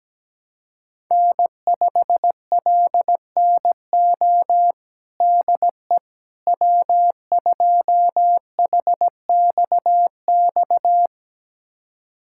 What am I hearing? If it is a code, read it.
Morse code, heard as a keyed tone, about 17 wpm, 700 Hz. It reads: N5LNO DE W2HXX
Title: USA Callsign DE Callsign - 5 Characters - 17wpm